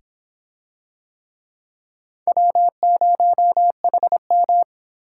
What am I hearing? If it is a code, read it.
Morse code, heard as a keyed tone, about 26 wpm, 700 Hz. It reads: W0HM